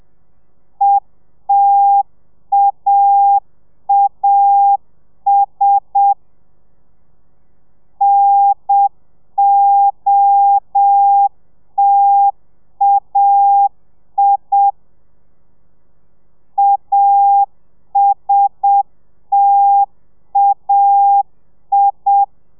Morse code, heard as a keyed tone, about 7 wpm, 800 Hz. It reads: ETAAS NOTAI ASTAI